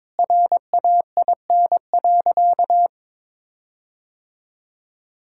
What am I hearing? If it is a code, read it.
Morse code, heard as a keyed tone, about 22 wpm, 700 Hz. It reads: RAIN.
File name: practice / 22wpm / Sentences with 4-Word Phrases - 22wpm